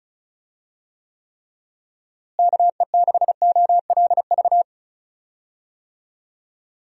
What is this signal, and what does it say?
Morse code, heard as a keyed tone, about 36 wpm, 700 Hz. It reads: KE6OLV
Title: USA Call Signs - 6 Characters - Repeated 3x - 35wpm